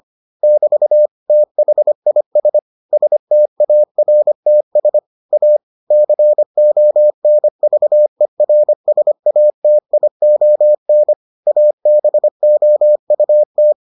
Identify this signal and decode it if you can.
Morse code, heard as a keyed tone, about 25 wpm, 600 Hz. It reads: <BT> THIS STARTS A CONVERSATION ABOUT